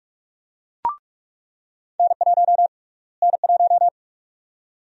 Morse code, heard as a keyed tone, about 45 wpm, 700 Hz. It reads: N1 N1
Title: USA Call Sign Prefixes - Repeated 2x - 45wpm